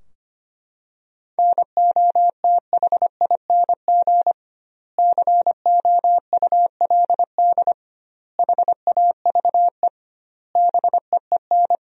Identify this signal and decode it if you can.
Morse code, heard as a keyed tone, about 25 wpm, 700 Hz. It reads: NOTHING COULD HAVE BEEN
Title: 2x Word-Spacing - Sentences from Top 500 Words - 25wpm